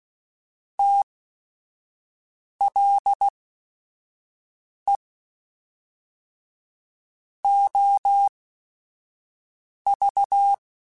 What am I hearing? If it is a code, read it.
Morse code, heard as a keyed tone, about 16 wpm, spaced as for 5 wpm, 800 Hz. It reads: TLE OV